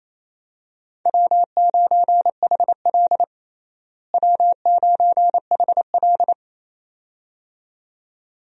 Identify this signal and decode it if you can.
Morse code, heard as a keyed tone, about 28 wpm, 700 Hz. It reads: W9HL W9HL
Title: USA Call Signs - 4 Characters - Repeated 2x - 28wpm